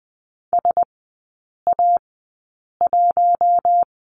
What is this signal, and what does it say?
Morse code, heard as a keyed tone, about 20 wpm, spaced as for 8 wpm, 700 Hz. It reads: SA1
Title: 081 Lesson 14e - Number 2 - Three Characters 20wpm